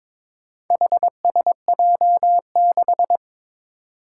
Morse code, heard as a keyed tone, about 22 wpm, 700 Hz. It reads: HSJ6
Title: Four Challenging Letter-Number Combinations 22wpm